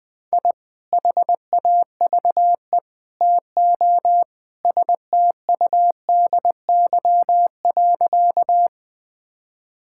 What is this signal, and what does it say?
Morse code, heard as a keyed tone, about 21 wpm, 700 Hz. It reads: I HAVE TO STUDY.
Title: Sentences with 3-Word Phrases - Spoken Beginning - 20wpm